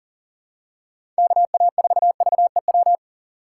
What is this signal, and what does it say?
Morse code, heard as a keyed tone, about 37 wpm, 700 Hz. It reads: KA4VEW
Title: USA Call Signs - 6 Characters - 4x Character Spacing - 40wpm